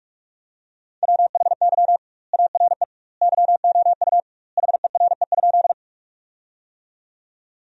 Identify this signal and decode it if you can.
Morse code, heard as a keyed tone, about 46 wpm, 700 Hz. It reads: WHY ARE YOU HERE?
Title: 2x Word-Spacing - Sentences from Top 200 Words - 45wpm